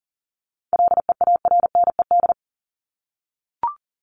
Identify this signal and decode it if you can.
Morse code, heard as a keyed tone, about 40 wpm, 700 Hz. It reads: LEARNED
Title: ED Suffix - Top 100 Words with Base Words 40wpm